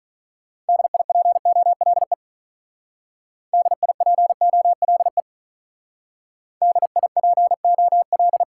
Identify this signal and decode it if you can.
Morse code, heard as a keyed tone, about 46 wpm, 700 Hz. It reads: DIPOLE DIPOLE DIPOL